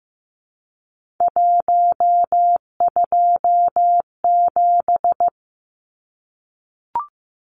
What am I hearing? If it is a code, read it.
Morse code, heard as a keyed tone, about 15 wpm, 700 Hz. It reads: 127
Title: Numbers - 3-Digit 15wpm